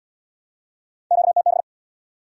Morse code, heard as a keyed tone, about 72 wpm, 700 Hz. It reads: QTB